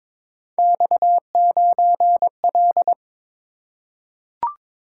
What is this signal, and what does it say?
Morse code, heard as a keyed tone, about 23 wpm, 700 Hz. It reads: X9L
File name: practice / 22wpm / Three Letter-Number Combinations - 22wpm